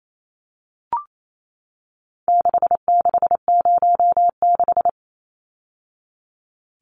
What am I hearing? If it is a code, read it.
Morse code, heard as a keyed tone, about 28 wpm, 700 Hz. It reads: 6606
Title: Numbers - 4-Digit 28wpm